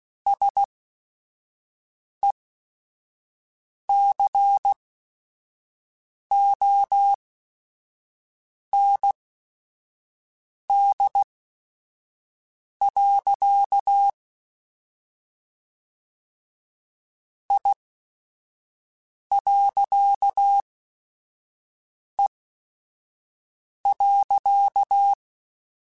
Morse code, heard as a keyed tone, about 16 wpm, spaced as for 5 wpm, 800 Hz. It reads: SECOND. I.E.
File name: Technical Subjects / Technical Subjects-mp3s / Reactance @5